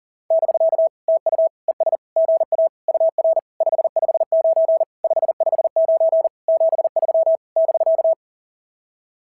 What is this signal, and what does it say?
Morse code, heard as a keyed tone, about 36 wpm, 660 Hz. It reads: <BK> TU ES GA UR 559 559 73 <BK>
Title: SOTA QSO Practice 40wpm